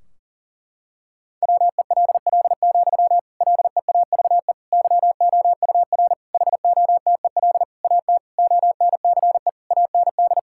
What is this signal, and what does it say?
Morse code, heard as a keyed tone, about 40 wpm, 700 Hz. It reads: WELL, LEAVE YOUR HOTEL AT ONCE AND
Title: Sentences from Top 1600 Words 40wpm